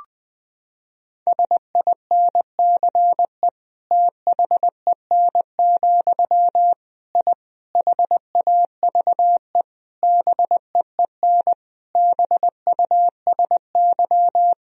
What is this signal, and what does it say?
Morse code, heard as a keyed tone, about 21 wpm, 700 Hz. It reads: SINCE THEN, I HAVE BEEN BUSY
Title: Sentences from Top 1000 Words - Review Part 3 - 20wpm